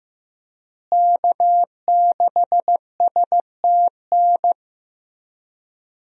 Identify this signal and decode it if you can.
Morse code, heard as a keyed tone, about 15 wpm, 700 Hz. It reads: K6STN
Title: USA Call Signs - 5 Characters - Repeated 2x - 15wpm